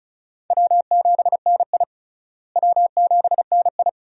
Morse code, heard as a keyed tone, about 34 wpm, 700 Hz. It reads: W7NI W7NI
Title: USA Call Signs - 4 Characters - Repeated 2x - 35wpm